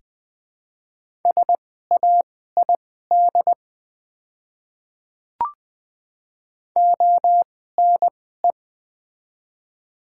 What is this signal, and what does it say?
Morse code, heard as a keyed tone, about 21 wpm, spaced as for 15 wpm, 700 Hz. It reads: SAID ONE
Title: 2x Character Spacing - Top 200 Words - 20wpm